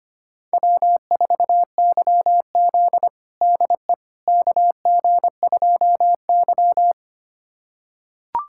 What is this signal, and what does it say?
Morse code, heard as a keyed tone, about 25 wpm, 700 Hz. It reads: W4YZ DE KG2Y
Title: USA Callsign DE Callsign - 4 Characters - 25wpm